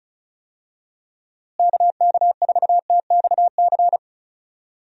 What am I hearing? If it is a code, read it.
Morse code, heard as a keyed tone, about 36 wpm, 700 Hz. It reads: KK4TXC
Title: USA Call Signs - 6 Characters - Repeated 3x - 35wpm